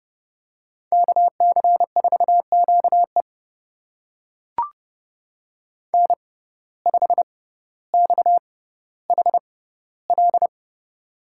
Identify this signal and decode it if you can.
Morse code, heard as a keyed tone, about 32 wpm, spaced as for 16 wpm, 700 Hz. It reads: KC4QE N5XHL
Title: USA Call Signs - 5 Characters - 6x Character Spacing - 30wpm